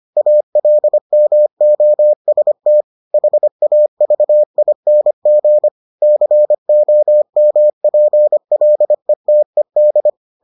Morse code, heard as a keyed tone, about 25 wpm, 600 Hz. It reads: ALMOST HAVING COMPLETED